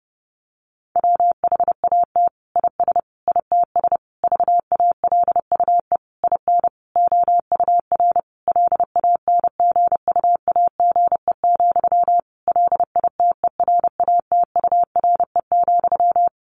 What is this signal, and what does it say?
Morse code, heard as a keyed tone, about 30 wpm, 700 Hz. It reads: WHAT IS ITS VALUE IN OUR LANGUAGE, LITERATURE,
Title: Sentences from Top 600 Words 30wpm